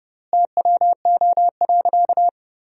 Morse code, heard as a keyed tone, about 32 wpm, 700 Hz. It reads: TWO.
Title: 3x Word-Spacing - Sentences from Top 200 Words - 30wpm